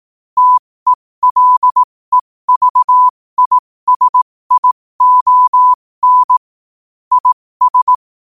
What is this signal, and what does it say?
Morse code, heard as a keyed tone, about 18 wpm, spaced as for 15 wpm, 1000 Hz. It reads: TELEVISION IS
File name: qotdfiles / 15wpm